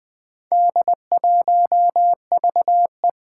Morse code, heard as a keyed tone, about 20 wpm, 700 Hz. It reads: D1VE
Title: Four Challenging Letter-Number Combinations 20wpm